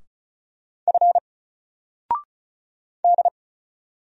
Morse code, heard as a keyed tone, about 36 wpm, 700 Hz. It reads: F D
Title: Single Letter-Number Alt 35wpm